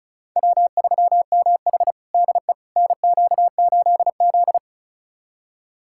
Morse code, heard as a keyed tone, about 36 wpm, 700 Hz. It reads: W3MH DE NQ8Z
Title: USA Callsign DE Callsign - 4 Characters - 35wpm